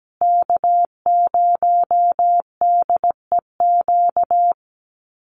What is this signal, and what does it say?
Morse code, heard as a keyed tone, about 17 wpm, 700 Hz. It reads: K0DEQ
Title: Call Signs Repeated 2x - 17wpm